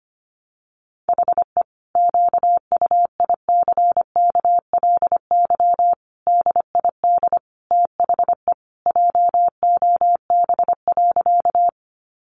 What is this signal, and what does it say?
Morse code, heard as a keyed tone, about 25 wpm, 700 Hz. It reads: HE QUICKLY DID THE JOB.
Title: Sentences from Top 700 Words 25wpm